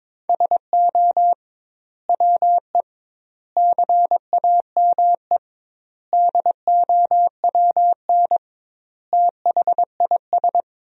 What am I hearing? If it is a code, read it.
Morse code, heard as a keyed tone, about 22 wpm, 700 Hz. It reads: SO WE CAME DOWN THIS
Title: 2x Word-Spacing - Sentences from Top 400 Words - 22wpm